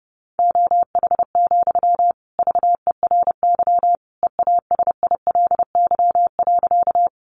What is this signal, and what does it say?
Morse code, heard as a keyed tone, about 30 wpm, 700 Hz. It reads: OH, VERY EASILY.
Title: Sentences from Top 900 Words 30wpm